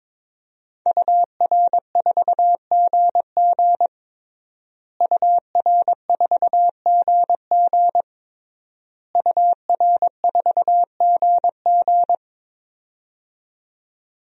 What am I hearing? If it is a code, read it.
Morse code, heard as a keyed tone, about 22 wpm, 700 Hz. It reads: UR4GG UR4GG UR4GG
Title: DXCC-100 Call Signs - 5 Characters - Repeated 3x - 22wpm